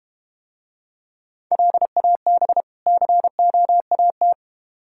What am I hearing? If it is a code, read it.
Morse code, heard as a keyed tone, about 32 wpm, 700 Hz. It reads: LAB COAT